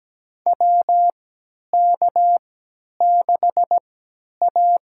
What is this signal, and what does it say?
Morse code, heard as a keyed tone, about 17 wpm, spaced as for 10 wpm, 700 Hz. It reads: WK6A
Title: USA Call Signs - 4 Characters - 3x Character Spacing - 17wpm